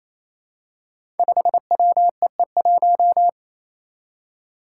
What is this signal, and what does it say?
Morse code, heard as a keyed tone, about 28 wpm, 700 Hz. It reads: 5WEE1
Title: Five Challenging Letter-Number Combinations 28wpm